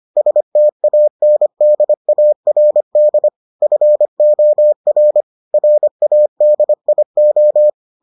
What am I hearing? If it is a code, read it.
Morse code, heard as a keyed tone, about 25 wpm, 600 Hz. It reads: STANDARD FOR RADIO